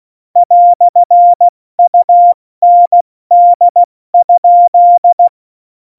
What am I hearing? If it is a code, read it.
Morse code, heard as a keyed tone, about 16 wpm, 700 Hz. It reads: "UND?
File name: Morsekurs / Witze / Witz 08. Januar 2019, 70 F80